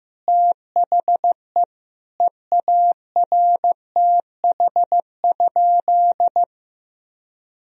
Morse code, heard as a keyed tone, about 15 wpm, 700 Hz. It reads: THE EARTH?
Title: Sentences from Top 1000 Words - Review Part 3 - 15wpm